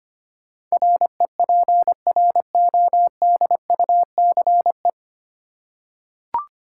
Morse code, heard as a keyed tone, about 25 wpm, 700 Hz. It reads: REPRODUCE
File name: Nine Letter Words - 25wpm